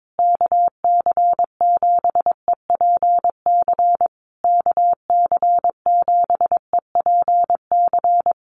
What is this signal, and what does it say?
Morse code, heard as a keyed tone, about 22 wpm, 700 Hz. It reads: KC7EPC KC7EPC
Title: USA Call Signs Repeated 3x 22wpm